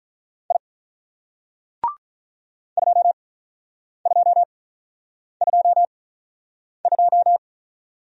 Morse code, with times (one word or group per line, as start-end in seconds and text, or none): 0.5-0.6 I
2.8-7.4 2222